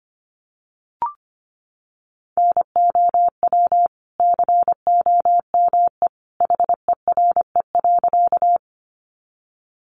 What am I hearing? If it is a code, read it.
Morse code, heard as a keyed tone, about 25 wpm, 700 Hz. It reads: NOW COME HERE.
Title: Sentences from Top 200 Words 25wpm